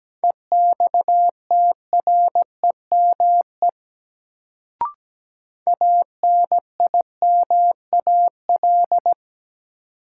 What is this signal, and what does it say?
Morse code, heard as a keyed tone, about 17 wpm, 700 Hz. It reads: EXTREME ANIMAL
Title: Sentences from Top 1300 Words - New Word Drill Alt - 17wpm